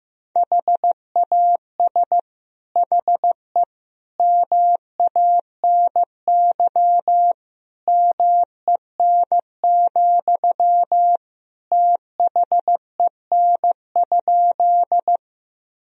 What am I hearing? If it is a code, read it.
Morse code, heard as a keyed tone, about 15 wpm, 700 Hz. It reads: HAS HE MANY MEN, THEN?
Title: Sentences from Top 1000 Words - Review Part 1 - 15wpm